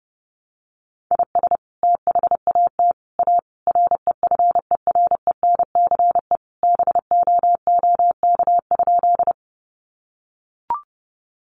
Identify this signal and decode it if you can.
Morse code, heard as a keyed tone, about 30 wpm, 700 Hz. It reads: IS THAT A REFERENCE BOOK?
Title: Sentences from Top 1200 Words 30wpm